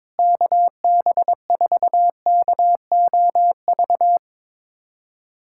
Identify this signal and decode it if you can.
Morse code, heard as a keyed tone, about 22 wpm, 700 Hz. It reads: KB4KOV
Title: USA Callsign DE Callsign - 6 Characters - 22wpm